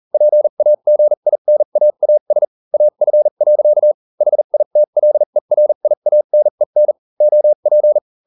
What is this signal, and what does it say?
Morse code, heard as a keyed tone, about 40 wpm, spaced as for 35 wpm, 600 Hz. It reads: PAGINAAS AF. HITLERIANEN OP